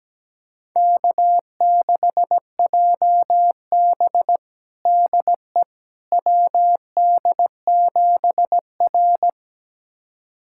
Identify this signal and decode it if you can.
Morse code, heard as a keyed tone, about 17 wpm, 700 Hz. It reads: K6JB DE WD7R
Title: USA Callsign DE Callsign - 4 Characters - 17wpm